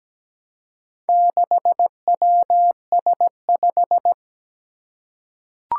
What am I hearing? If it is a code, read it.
Morse code, heard as a keyed tone, about 17 wpm, 700 Hz. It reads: 6WS5
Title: Four Challenging Letter-Number Combinations 17wpm